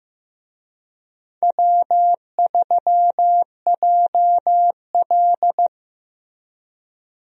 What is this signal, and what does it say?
Morse code, heard as a keyed tone, about 15 wpm, 700 Hz. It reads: W3JL
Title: USA Call Signs - 4 Characters - Repeated 3x - 15wpm